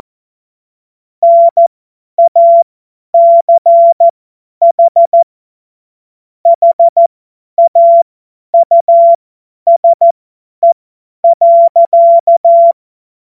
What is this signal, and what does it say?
Morse code, heard as a keyed tone, about 14 wpm, spaced as for 10 wpm, 680 Hz. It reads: NACH HAUSE.
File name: Morsekurs / Witze / Witz 22. Februar 2022 50 F70